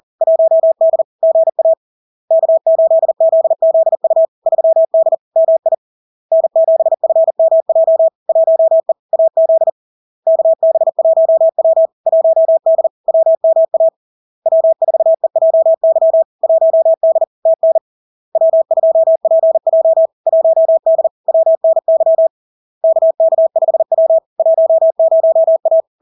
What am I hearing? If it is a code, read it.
Morse code, heard as a keyed tone, about 40 wpm, 650 Hz. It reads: 1D GA K8ZZU 3D MI N7FMJ 1E AZ KB1W 1D WMA W4EJY 1D TN W2PJ 1D WNY KK5W 10A
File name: LARK-FD-2022-Log-in-CW-40wpm-650-Hz